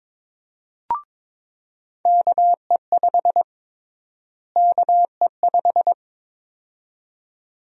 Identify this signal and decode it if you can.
Morse code, heard as a keyed tone, about 22 wpm, 700 Hz. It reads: KE5 KE5
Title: USA Call Sign Prefixes - Repeated 2x - 22wpm